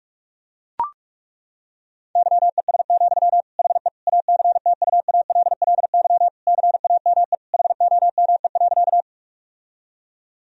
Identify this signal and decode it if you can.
Morse code, heard as a keyed tone, about 41 wpm, 700 Hz. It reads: YES, HE ACTUALLY CAME HOME.
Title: Sentences from Top 1000 Words - Review Part 4 - 45wpm